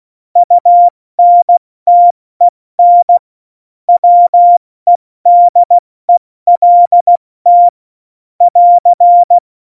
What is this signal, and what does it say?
Morse code, heard as a keyed tone, about 16 wpm, 700 Hz. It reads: UNTEN WEDELT <AR>
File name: Morsekurs / Witze / Witz 25. Dezember, 70 F80